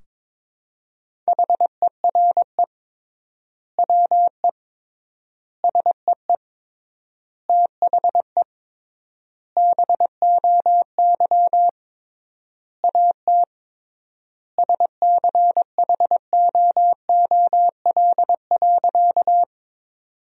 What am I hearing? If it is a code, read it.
Morse code, heard as a keyed tone, about 22 wpm, 700 Hz. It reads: HERE WE SEE THE BOY AT SCHOOL.
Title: 3x Word-Spacing - Sentences from Top 300 Words - 22wpm